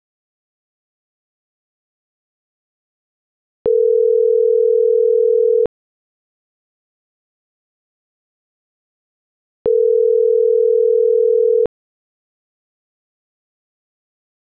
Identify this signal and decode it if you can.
Ringback tone (North America), 440 + 480 Hz.